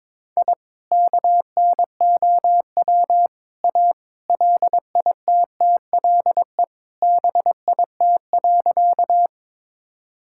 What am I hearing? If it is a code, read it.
Morse code, heard as a keyed tone, about 22 wpm, 700 Hz. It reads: I KNOW A LITTLE BIT.
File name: practice / 22wpm / Sentences with 3-Word Phrases - 22wpm